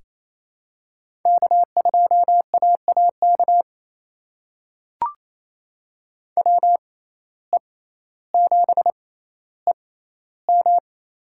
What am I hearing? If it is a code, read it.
Morse code, heard as a keyed tone, about 28 wpm, spaced as for 14 wpm, 700 Hz. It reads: K2AAK WE7EM
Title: USA Call Signs - 5 Characters - 6x Character Spacing - 28wpm